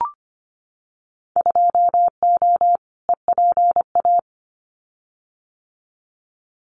Morse code, heard as a keyed tone, about 25 wpm, 700 Hz. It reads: 2O EPA
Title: Winter Field Day Exchanges Encore 25wpm